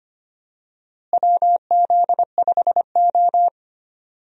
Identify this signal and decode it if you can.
Morse code, heard as a keyed tone, about 25 wpm, 700 Hz. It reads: WZ5O
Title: USA Call Signs - 4 Characters - Repeated 3x - 25wpm